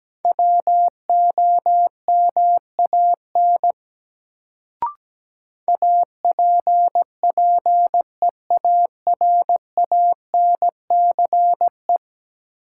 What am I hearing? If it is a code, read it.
Morse code, heard as a keyed tone, about 17 wpm, 700 Hz. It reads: WOMAN APPEARANCE
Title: Sentences from Top 1100 Words - New Word Drill Alt - 17wpm